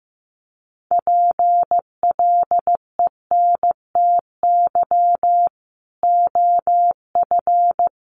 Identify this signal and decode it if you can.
Morse code, heard as a keyed tone, about 15 wpm, 700 Hz. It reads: PLENTY OF
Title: Sentences from Top 1300 Words 15wpm